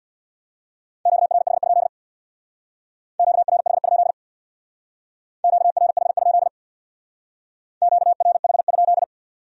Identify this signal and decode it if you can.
Morse code, heard as a keyed tone, about 62 wpm, 700 Hz. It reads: QRH? QRH? QRH? QRH?
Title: Speed-Racing Q-Codes - 50wpm